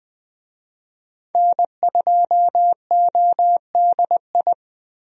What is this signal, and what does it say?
Morse code, heard as a keyed tone, about 20 wpm, 700 Hz. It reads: N2ODI
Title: USA Call Signs - 5 Characters - 20wpm